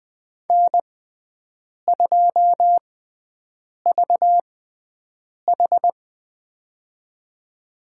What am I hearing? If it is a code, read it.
Morse code, heard as a keyed tone, about 20 wpm, spaced as for 7 wpm, 700 Hz. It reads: N2VH
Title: USA Call Signs - 4 Characters - 6x Character Spacing - 20wpm